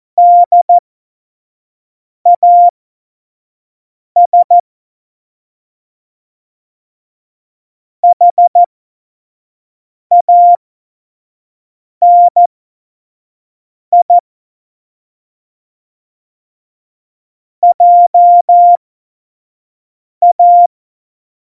Morse code, heard as a keyed tone, about 14 wpm, spaced as for 5 wpm, 700 Hz. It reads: DAS HANI JA
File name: Morsekurs / Witze / Witz 24. November 2020, 25 F70